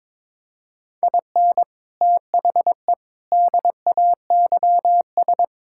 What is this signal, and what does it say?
Morse code, heard as a keyed tone, about 22 wpm, 700 Hz. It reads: IN THE DAYS
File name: Sentences with 7-Word Phrases - 22wpm